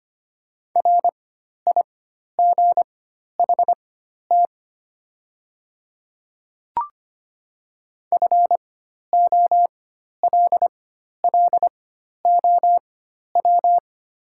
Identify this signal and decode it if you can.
Morse code, heard as a keyed tone, about 25 wpm, spaced as for 12 wpm, 700 Hz. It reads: RIGHT FOLLOW
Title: 4x Character Spacing - Top 200 Words - 25wpm